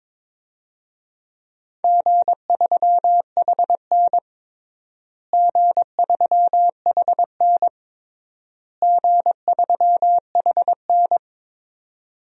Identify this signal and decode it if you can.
Morse code, heard as a keyed tone, about 22 wpm, 700 Hz. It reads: G3HN G3HN G3HN
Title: DXCC-100 Call Signs - 4 Characters - Repeated 3x - 22wpm